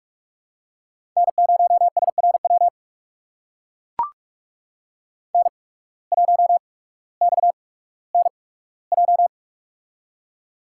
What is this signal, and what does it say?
Morse code, heard as a keyed tone, about 45 wpm, spaced as for 19 wpm, 700 Hz. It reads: N0SRW N1XNJ
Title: USA Call Signs - 5 Characters - 8x Character Spacing - 45wpm